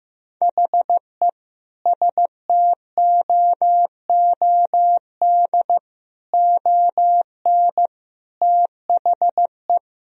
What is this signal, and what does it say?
Morse code, heard as a keyed tone, about 15 wpm, 700 Hz. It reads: HE STOOD ON THE
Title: Sentences from Top 1000 Words - Review Part 4 - 15wpm